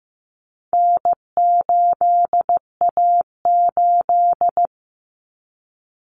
Morse code, heard as a keyed tone, about 15 wpm, 700 Hz. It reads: N8A8
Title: Common Cut Numbers - 4-Digit 15wpm